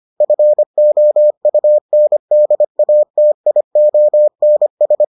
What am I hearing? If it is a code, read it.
Morse code, heard as a keyed tone, about 25 wpm, 600 Hz. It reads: FOUNDATIONS